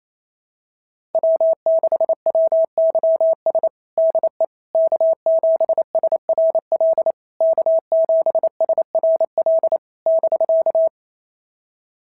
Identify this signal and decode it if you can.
Morse code, heard as a keyed tone, about 25 wpm, 650 Hz. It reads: W6WYS DE K7SRL K7SRL <BK>
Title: SOTA QSO Practice 28wpm